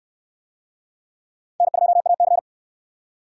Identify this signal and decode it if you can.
Morse code, heard as a keyed tone, about 66 wpm, 700 Hz. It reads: N2AC